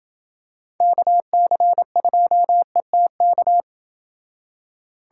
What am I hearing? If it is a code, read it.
Morse code, heard as a keyed tone, about 28 wpm, 700 Hz. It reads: KC2ETK